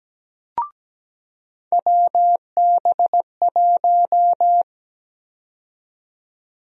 Morse code, heard as a keyed tone, about 17 wpm, 700 Hz. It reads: WB1
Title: USA Call Sign Prefixes - 17wpm